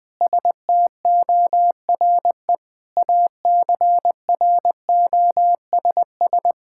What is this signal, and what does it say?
Morse code, heard as a keyed tone, about 20 wpm, 700 Hz. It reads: STORE ACROSS
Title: Sentences from Top 1000 Words - Review Part 5 - 20wpm